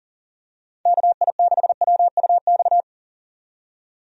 Morse code, heard as a keyed tone, about 39 wpm, 700 Hz. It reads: KI6WUX
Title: USA Call Signs - 6 Characters - 40wpm